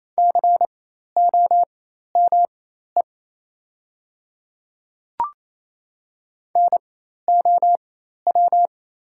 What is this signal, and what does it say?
Morse code, heard as a keyed tone, about 28 wpm, spaced as for 13 wpm, 690 Hz. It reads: COME NOW
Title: 4x Character Spacing - Top 100 Words - 28wpm